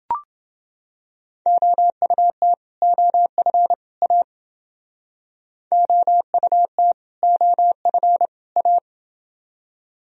Morse code, heard as a keyed tone, about 28 wpm, 700 Hz. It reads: OUT OF A OUT OF A